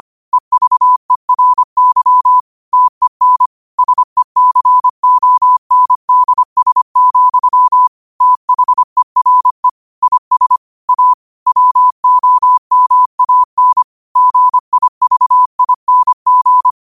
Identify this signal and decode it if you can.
Morse code, heard as a keyed tone, about 25 wpm, 1000 Hz. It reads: EVERY TEN SECONDS, THERE IS A WOMAN GIVING